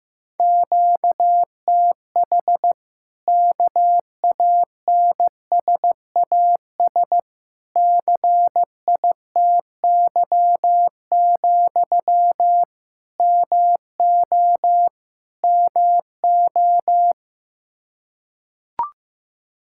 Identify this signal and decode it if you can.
Morse code, heard as a keyed tone, about 15 wpm, 700 Hz. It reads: QTH KANSAS CITY, MO MO
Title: Practice QSO - Exchange 1 - Beginning - 15wpm